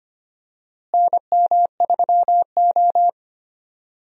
Morse code, heard as a keyed tone, about 25 wpm, 700 Hz. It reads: NM3O
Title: USA Call Signs - 4 Characters - Repeated 3x - 25wpm